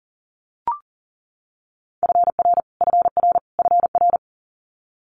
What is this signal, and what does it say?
Morse code, heard as a keyed tone, about 42 wpm, 700 Hz. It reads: FR FR FR
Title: Multi-letter Phonemes Repeated 3x 40wpm